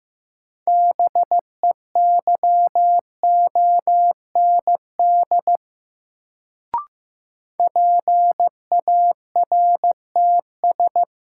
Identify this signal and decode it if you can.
Morse code, heard as a keyed tone, about 15 wpm, 700 Hz. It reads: BEYOND PARTS
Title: Sentences from Top 1000 Words - Review Part 6 - New Word Drill - 15wpm